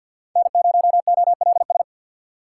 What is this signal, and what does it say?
Morse code, heard as a keyed tone, about 47 wpm, 700 Hz. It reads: N0OLS
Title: USA Call Signs - 4x Character Spacing - 50wpm